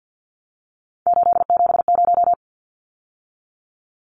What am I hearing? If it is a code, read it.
Morse code, heard as a keyed tone, about 49 wpm, 700 Hz. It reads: 870
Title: Numbers - 3-Digit 50wpm